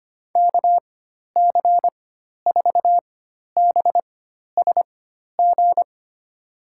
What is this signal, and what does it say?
Morse code, heard as a keyed tone, about 25 wpm, spaced as for 12 wpm, 700 Hz. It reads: KC4BSG
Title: USA Call Signs - 6 Characters - 4x Character Spacing - 25wpm